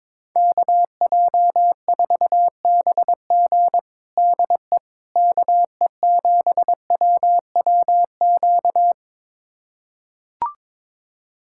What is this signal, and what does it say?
Morse code, heard as a keyed tone, about 22 wpm, 700 Hz. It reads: KJ4BG DE KE7WWQ
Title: USA Callsign DE Callsign 22wpm